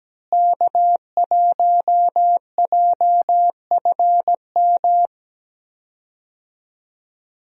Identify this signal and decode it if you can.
Morse code, heard as a keyed tone, about 17 wpm, 700 Hz. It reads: K1JFM
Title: USA Call Signs - 5 Characters - Repeated 2x - 17wpm